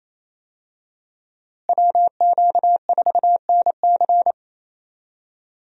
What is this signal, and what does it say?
Morse code, heard as a keyed tone, about 28 wpm, 700 Hz. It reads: WQ4NC
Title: USA Call Signs - 5 Characters - Repeated 3x - 28wpm